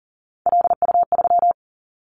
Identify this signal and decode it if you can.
Morse code, heard as a keyed tone, about 39 wpm, 700 Hz. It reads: LU3
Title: Callsign Prefixes for Top 100 DXCC - 40wpm